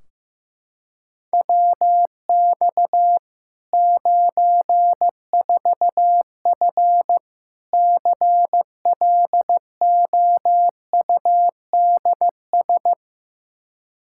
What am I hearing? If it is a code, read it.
Morse code, heard as a keyed tone, about 15 wpm, 700 Hz. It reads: WX 94F CLOUDS
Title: Practice QSO - Exchange 2 - Info - 15wpm